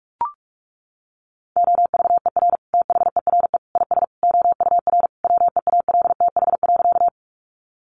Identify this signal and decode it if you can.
Morse code, heard as a keyed tone, about 44 wpm, 700 Hz. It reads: OVER THERE IS OUR WEALTH.
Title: Sentences from Top 1200 Words 45wpm